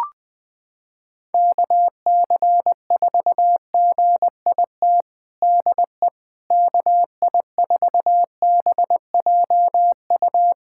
Morse code, heard as a keyed tone, about 20 wpm, 700 Hz. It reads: KC4GIT DE KI4BJU
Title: USA Callsign DE Callsign 20wpm